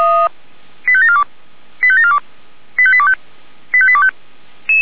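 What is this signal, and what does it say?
Five-tone selective call (ZVEI), heard as quick runs of five tones, tones of about 70 ms. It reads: 85742 85742 85825 85825
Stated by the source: ZVEI five-tone selective call received with an SDR.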